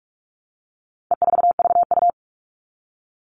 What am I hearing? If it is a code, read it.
Morse code, heard as a keyed tone, about 44 wpm, 700 Hz. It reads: E4VU
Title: All Cut Numbers - 4-Digit 45wpm